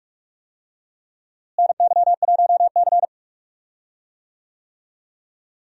Morse code, heard as a keyed tone, about 46 wpm, 700 Hz. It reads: NY1C
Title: USA Call Signs - 4 Characters - Repeated 2x - 45wpm